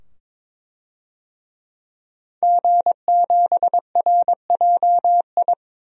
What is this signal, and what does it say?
Morse code, heard as a keyed tone, about 22 wpm, 700 Hz. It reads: G7RJI
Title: DXCC-100 Call Signs - 5 Characters - 22wpm